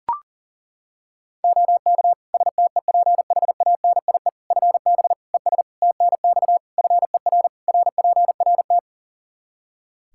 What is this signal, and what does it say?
Morse code, heard as a keyed tone, about 37 wpm, 700 Hz. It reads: OK STEPHANIE FB ES TNX FER RPRT
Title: Practice QSO - Exchange 2 - Info - 40wpm